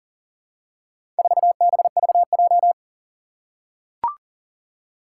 Morse code, heard as a keyed tone, about 40 wpm, 700 Hz. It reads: 4BVJ
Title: Four Challenging Letter-Number Combinations 40wpm